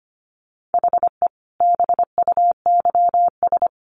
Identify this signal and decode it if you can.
Morse code, heard as a keyed tone, about 25 wpm, 700 Hz. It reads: HE BUYS
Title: Additional Easy Phrases 25wpm